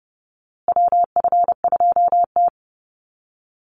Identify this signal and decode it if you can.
Morse code, heard as a keyed tone, about 30 wpm, 700 Hz. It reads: WF2T
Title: USA Call Signs Repeated 2x 30wpm